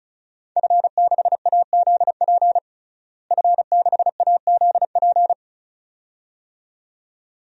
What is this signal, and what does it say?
Morse code, heard as a keyed tone, about 36 wpm, 700 Hz. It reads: F6AZP F6AZP
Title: DXCC-100 Call Signs - 5 Characters - Repeated 2x - 35wpm